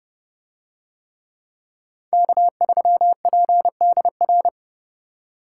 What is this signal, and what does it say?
Morse code, heard as a keyed tone, about 28 wpm, 700 Hz. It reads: K3PDR